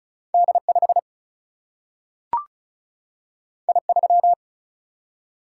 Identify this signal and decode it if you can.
Morse code, heard as a keyed tone, about 36 wpm, 700 Hz. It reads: D5 I3
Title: Single-Letter Single-Number 35wpm